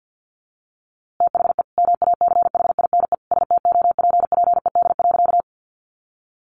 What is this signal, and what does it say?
Morse code, heard as a keyed tone, about 50 wpm, 700 Hz. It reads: THE MACHINE STOPPED.
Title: A Sound Of Thunder 50wpm